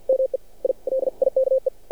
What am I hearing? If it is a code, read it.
Morse code, heard as a keyed tone, about 50 wpm, 530 Hz. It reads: ME I LIKE
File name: archived / CRSnet